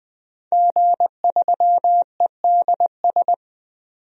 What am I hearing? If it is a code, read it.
Morse code, heard as a keyed tone, about 20 wpm, 700 Hz. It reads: G3EDS
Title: DXCC-100 Call Signs - 5 Characters - 20wpm